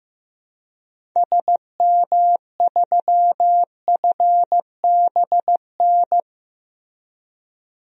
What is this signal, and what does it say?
Morse code, heard as a keyed tone, about 15 wpm, 700 Hz. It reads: SM3FBN
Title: DXCC-100 Call Signs - 6 Characters - Repeated 2x - 15wpm